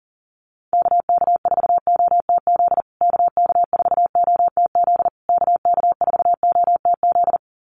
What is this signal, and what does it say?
Morse code, heard as a keyed tone, about 37 wpm, 700 Hz. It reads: KK4OTZ KK4OTZ KK4OTZ
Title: USA Call Signs Repeated 3x 40wpm